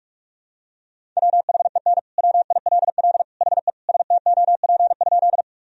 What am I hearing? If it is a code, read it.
Morse code, heard as a keyed tone, about 42 wpm, 700 Hz. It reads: WHEN WILL HE STOP?
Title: Sentences from Top 1000 Words - Review Part 1 - 45wpm